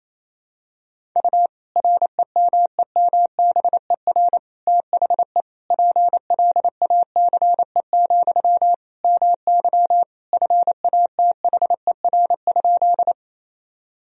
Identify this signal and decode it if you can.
Morse code, heard as a keyed tone, about 28 wpm, 700 Hz. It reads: U REMEMBER THE PLACE, MY FATHER?
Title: Sentences from Top 400 Words and Abbreviations 28wpm